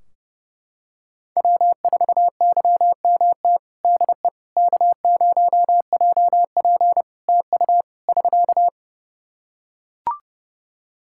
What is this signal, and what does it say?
Morse code, heard as a keyed tone, about 28 wpm, 700 Hz. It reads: W4YMT DE K0JP TU <SK>
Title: Practice QSO - Exchange 4 - Ending - 30wpm